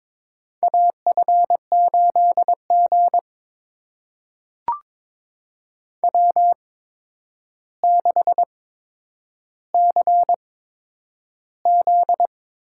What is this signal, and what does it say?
Morse code, heard as a keyed tone, about 22 wpm, spaced as for 9 wpm, 700 Hz. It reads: AF8G W6CZ
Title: USA Call Signs - 4 Characters - 8x Character Spacing - 22wpm